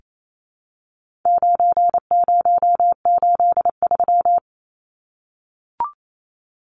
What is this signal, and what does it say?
Morse code, heard as a keyed tone, about 28 wpm, 700 Hz. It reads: 9083
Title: Numbers - 4-Digit 28wpm